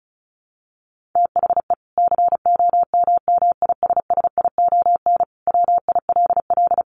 Morse code, heard as a keyed tone, about 34 wpm, 700 Hz. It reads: THE COMMISSION WILL
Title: Sentences from Top 1400 Words 35wpm